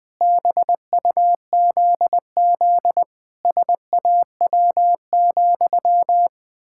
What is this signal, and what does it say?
Morse code, heard as a keyed tone, about 20 wpm, 700 Hz. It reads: BUZZ SAW,
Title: Pangram Challenge v2 - 20wpm